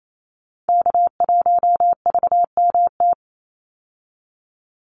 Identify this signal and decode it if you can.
Morse code, heard as a keyed tone, about 28 wpm, 700 Hz. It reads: K1VMT
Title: Call Signs Repeated 2x - 28wpm